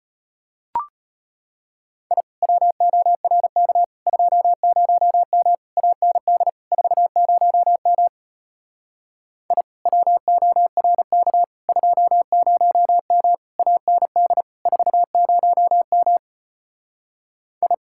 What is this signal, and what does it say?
Morse code, heard as a keyed tone, about 33 wpm, 700 Hz. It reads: I WORK 20M AND 40M I WORK 20M AND 40M I